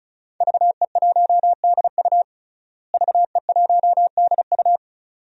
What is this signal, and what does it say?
Morse code, heard as a keyed tone, about 34 wpm, 700 Hz. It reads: VE1DU VE1DU
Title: DXCC-100 Call Signs - 5 Characters - Repeated 2x - 35wpm